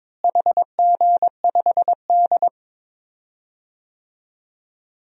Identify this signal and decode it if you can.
Morse code, heard as a keyed tone, about 22 wpm, 700 Hz. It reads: HG5D
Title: DXCC-100 Call Signs - 4 Characters - Repeated 3x - 22wpm